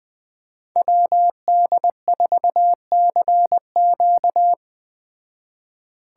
Morse code, heard as a keyed tone, about 20 wpm, 700 Hz. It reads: WD4CQ
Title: Speed Racing - USA 2x2 Call Signs 17wpm